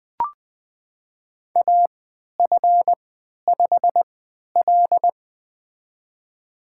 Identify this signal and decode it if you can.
Morse code, heard as a keyed tone, about 20 wpm, spaced as for 11 wpm, 700 Hz. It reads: AF5L
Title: USA Call Signs - 3x Character Spacing - 20wpm